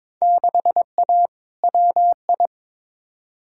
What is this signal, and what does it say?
Morse code, heard as a keyed tone, about 22 wpm, 700 Hz. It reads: 6A WI
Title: Field Day Exchanges - 22wpm